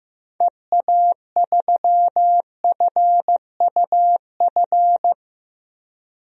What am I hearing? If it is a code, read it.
Morse code, heard as a keyed tone, about 15 wpm, 700 Hz. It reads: EA3FUF
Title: DXCC-100 Call Signs - 6 Characters - Repeated 3x - 15wpm